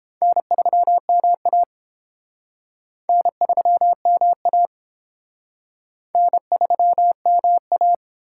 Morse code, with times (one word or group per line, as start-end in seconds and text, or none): 0.2-1.6 N3MA
3.1-4.7 N3MA
6.2-7.9 N3MA